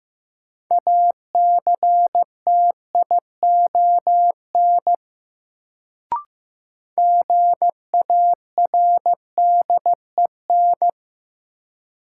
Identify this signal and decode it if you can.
Morse code, heard as a keyed tone, about 16 wpm, 700 Hz. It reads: ACTION GARDEN
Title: Sentences from Top 700 Words - New Word Drill - 15wpm